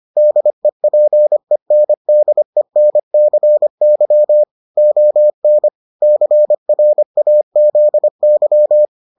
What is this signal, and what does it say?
Morse code, heard as a keyed tone, about 25 wpm, 600 Hz. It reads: DEPENDENCY ON CRAZY